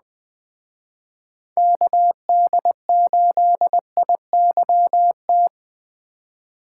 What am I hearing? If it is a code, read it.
Morse code, heard as a keyed tone, about 20 wpm, 700 Hz. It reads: KD8IYT